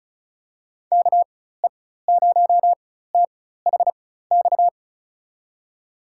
Morse code, heard as a keyed tone, about 36 wpm, spaced as for 16 wpm, 700 Hz. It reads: KE0THX
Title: USA Call Signs - 6 Characters - 4x Character Spacing - 35wpm